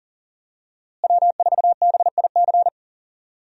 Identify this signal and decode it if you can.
Morse code, heard as a keyed tone, about 40 wpm, 700 Hz. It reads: W4BIC
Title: USA Call Signs - 5 Characters - Repeated 3x - 40wpm